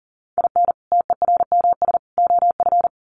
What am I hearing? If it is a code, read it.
Morse code, heard as a keyed tone, about 40 wpm, 700 Hz. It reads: IN TERMS OF
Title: Sets of 3 Words 40wpm